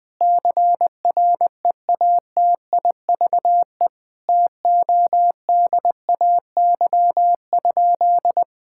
Morse code, heard as a keyed tone, about 20 wpm, 700 Hz. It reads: CREATIVE TODAY?
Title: Sentences from Top 1000 Words - Review Part 5 - 20wpm